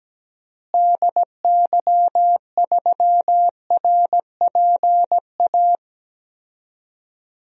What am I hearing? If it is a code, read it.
Morse code, heard as a keyed tone, about 17 wpm, 700 Hz. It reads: DY3RPA
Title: DXCC-100 Call Signs - 6 Characters - Repeated 2x - 17wpm